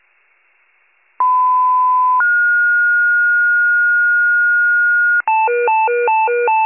Two-tone page, heard as a one-second tone, then a three-second tone, about 1010 then 1430 Hz. It is heard from 1.2 to 5.2 s.